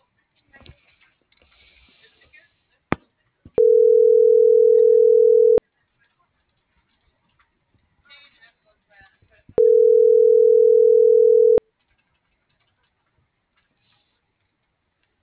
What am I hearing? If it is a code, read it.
Ringback tone (North America), 440 + 480 Hz.